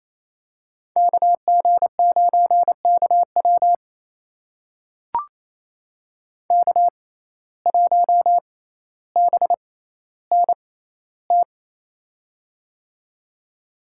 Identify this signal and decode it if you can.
Morse code, heard as a keyed tone, about 28 wpm, spaced as for 14 wpm, 700 Hz. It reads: KG9KW K1BNT
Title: USA Call Signs - 5 Characters - 6x Character Spacing - 28wpm